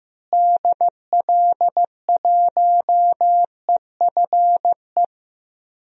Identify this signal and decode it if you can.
Morse code, heard as a keyed tone, about 15 wpm, 700 Hz. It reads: DL1EFE
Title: DXCC-100 Call Signs - 6 Characters - Repeated 2x - 15wpm